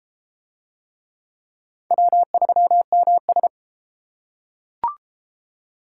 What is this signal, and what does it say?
Morse code, heard as a keyed tone, about 32 wpm, 700 Hz. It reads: W3MS